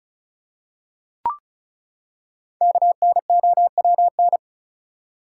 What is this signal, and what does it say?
Morse code, heard as a keyed tone, about 36 wpm, 700 Hz. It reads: KNOWN
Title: Five Letter Words - 35wpm